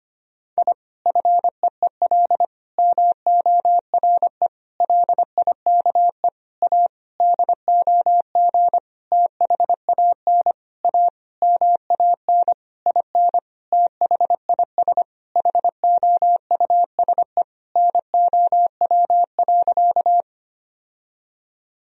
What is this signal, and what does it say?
Morse code, heard as a keyed tone, about 25 wpm, 700 Hz. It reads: I FEEL MORE LIKE A DOG THAN A MAN IN THIS HOUSE NOW.
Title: Sentences from Top 1000 Words - Review Part 2 - 25wpm